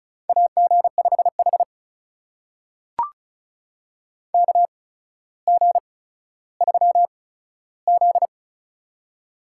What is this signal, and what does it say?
Morse code, heard as a keyed tone, about 36 wpm, spaced as for 15 wpm, 700 Hz. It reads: AG5H KG3Z
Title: USA Call Signs - 4 Characters - 8x Character Spacing - 35wpm